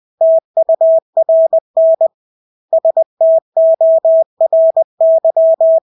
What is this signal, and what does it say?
Morse code, heard as a keyed tone, about 20 wpm, 650 Hz. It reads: TURN STORY
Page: Engelske ord 20 wpm